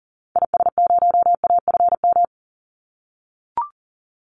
Morse code, heard as a keyed tone, about 40 wpm, 700 Hz. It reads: IS0AFM
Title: Call Signs - 40wpm